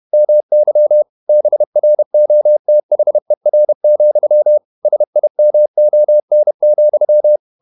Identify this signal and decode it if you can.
Morse code, heard as a keyed tone, about 31 wpm, 600 Hz. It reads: MY BROTHER, SIMON,
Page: Never 31wpm | CW med Gnister